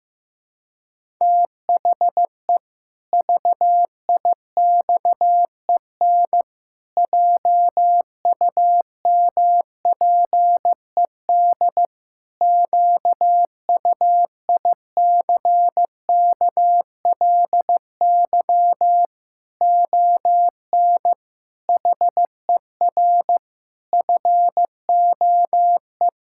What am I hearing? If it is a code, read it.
Morse code, heard as a keyed tone, about 16 wpm, 700 Hz. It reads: THE VIXEN JUMPED QUICKLY ON HER FOE
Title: Pangram Challenge v2 - 15wpm